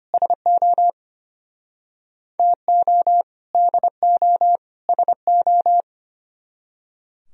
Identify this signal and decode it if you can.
Morse code, heard as a keyed tone, about 32 wpm, 700 Hz. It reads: SO TO DO SO